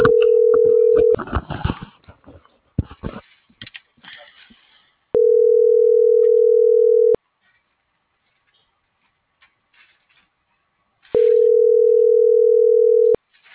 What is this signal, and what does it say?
Ringback tone (North America), 440 + 480 Hz.